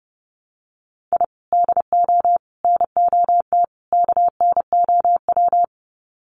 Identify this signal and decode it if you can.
Morse code, heard as a keyed tone, about 30 wpm, 700 Hz. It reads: I DO NOT KNOW
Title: Sets of 4 Words 30wpm